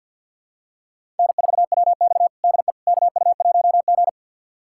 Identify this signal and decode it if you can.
Morse code, heard as a keyed tone, about 46 wpm, 700 Hz. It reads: N4WX DE KU1G
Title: USA Callsign DE Callsign - 4 Characters - 50wpm